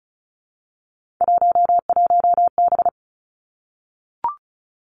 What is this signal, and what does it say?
Morse code, heard as a keyed tone, about 36 wpm, 700 Hz. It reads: 11B
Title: Three Challenging Letter-Number Combinations 35wpm